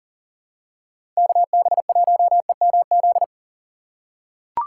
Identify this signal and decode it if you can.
Morse code, heard as a keyed tone, about 40 wpm, 700 Hz. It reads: KB1EMZ
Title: USA Call Signs - 6 Characters - 6x Character Spacing - 40wpm